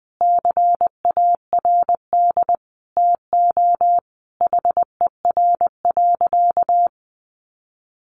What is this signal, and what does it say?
Morse code, heard as a keyed tone, about 20 wpm, 700 Hz. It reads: CARD TO HER.
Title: Sentences from Top 600 Words 20wpm